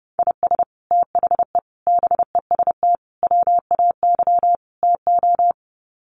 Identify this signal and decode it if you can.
Morse code, heard as a keyed tone, about 30 wpm, 700 Hz. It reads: IS THE BEST WAY TO
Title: Sets of 5 Words Encore 30wpm